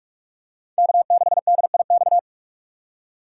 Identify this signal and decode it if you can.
Morse code, heard as a keyed tone, about 46 wpm, 700 Hz. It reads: K6DIX
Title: USA Call Signs - 5 Characters - 4x Character Spacing - 45wpm